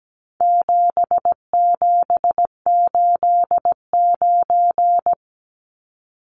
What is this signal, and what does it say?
Morse code, heard as a keyed tone, about 17 wpm, 700 Hz. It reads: 7789
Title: Numbers - 4-Digit 17wpm